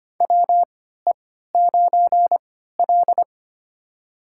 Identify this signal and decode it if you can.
Morse code, heard as a keyed tone, about 25 wpm, spaced as for 14 wpm, 700 Hz. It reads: WE9L
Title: USA Call Signs - 4 Characters - 3x Character Spacing - 25wpm